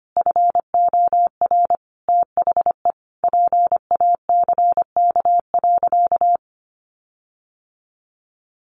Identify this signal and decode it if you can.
Morse code, heard as a keyed tone, about 25 wpm, 700 Hz. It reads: FOR THE PACK.
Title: Sentences from Top 1300 Words 25wpm